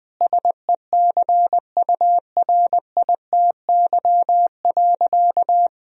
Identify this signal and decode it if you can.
Morse code, heard as a keyed tone, about 20 wpm, 700 Hz. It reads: SECURITY.
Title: Sentences from Top 1000 Words - Review Part 4 - 20wpm